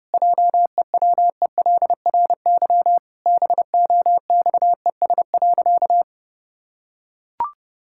Morse code, heard as a keyed tone, about 30 wpm, 700 Hz. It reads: JEWELRY BOXES.
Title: Pangram Challenge v2 - 30wpm